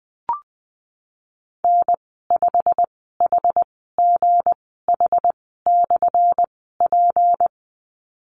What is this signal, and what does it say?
Morse code, heard as a keyed tone, about 20 wpm, spaced as for 15 wpm, 700 Hz. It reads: N5HGH/P
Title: 179 Lesson 29g - Slash - Call Signs 20wpm